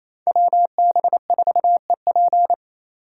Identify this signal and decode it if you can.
Morse code, heard as a keyed tone, about 34 wpm, 700 Hz. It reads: WB4EP
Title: Speed Racing - USA 2x2 Call Signs 28wpm